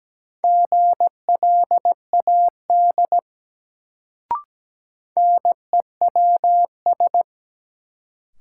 Morse code, heard as a keyed tone, about 17 wpm, 700 Hz. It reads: GLAD NEWS
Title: Sentences from Top 800 Words - New Word Drill - 17wpm